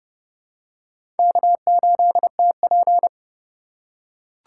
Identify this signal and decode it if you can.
Morse code, heard as a keyed tone, about 32 wpm, 690 Hz. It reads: K8TP
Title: USA Call Signs - 3x Character Spacing - 30wpm